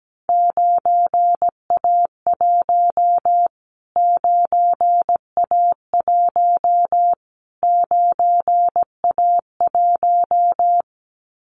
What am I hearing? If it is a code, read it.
Morse code, heard as a keyed tone, about 17 wpm, 700 Hz. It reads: 9A1 9A1 9A1
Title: Callsign Prefixes for Top 25 DXCC - Repeated 3x - 17wpm